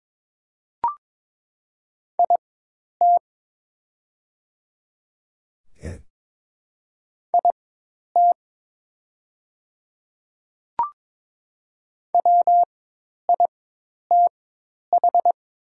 Morse code, heard as a keyed tone, about 22 wpm, spaced as for 10 wpm, 700 Hz. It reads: IT IT WITH
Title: 4x Character Spacing - Top 200 Words - 22wpm